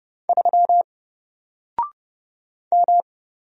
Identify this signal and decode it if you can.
Morse code, heard as a keyed tone, about 29 wpm, 700 Hz. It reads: 3 M
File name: Single-Letter-Number-30wpm